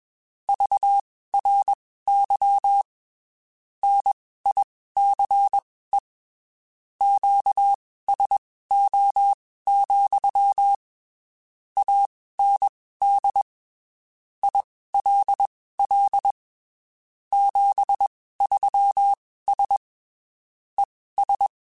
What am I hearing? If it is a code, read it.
Morse code, heard as a keyed tone, about 21 wpm, spaced as for 15 wpm, 800 Hz. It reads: VRY NICE QSO, AND ILL 73S ES